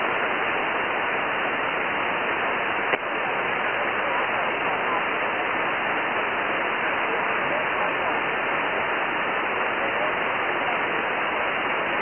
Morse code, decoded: none